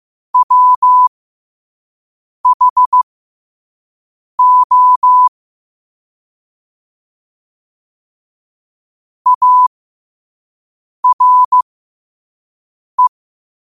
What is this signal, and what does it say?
Morse code, heard as a keyed tone, about 15 wpm, spaced as for 5 wpm, 1000 Hz. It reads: WHO ARE
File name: qotdfiles / 5wpm